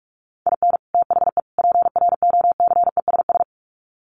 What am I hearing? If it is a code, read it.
Morse code, heard as a keyed tone, about 44 wpm, 700 Hz. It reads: IN THE PROCESS
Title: Sets of 3 Words 45wpm